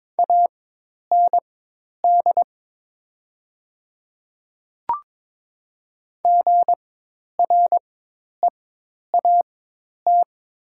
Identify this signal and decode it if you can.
Morse code, heard as a keyed tone, about 22 wpm, spaced as for 10 wpm, 700 Hz. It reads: AND GREAT
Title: 4x Character Spacing - Top 200 Words - 22wpm